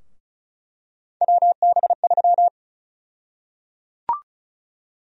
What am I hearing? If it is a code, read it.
Morse code, heard as a keyed tone, about 34 wpm, 700 Hz. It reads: WB3
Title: USA Call Sign Prefixes - Repeated 2x - 35wpm